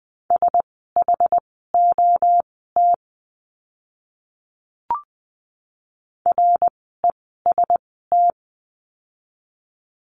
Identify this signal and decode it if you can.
Morse code, heard as a keyed tone, about 20 wpm, spaced as for 15 wpm, 700 Hz. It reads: SHOT REST
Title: 061 Lesson 11f - Letter h - Words 20wpm